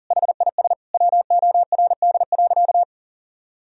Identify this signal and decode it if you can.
Morse code, heard as a keyed tone, about 40 wpm, 700 Hz. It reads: HIS WORD.
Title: Sentences from Top 1000 Words - Review Part 1 - 40wpm